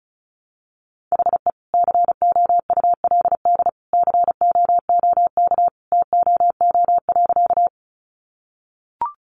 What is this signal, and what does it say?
Morse code, heard as a keyed tone, about 34 wpm, 700 Hz. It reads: HE COULD COOK TOO.
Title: Sentences from Top 700 Words 35wpm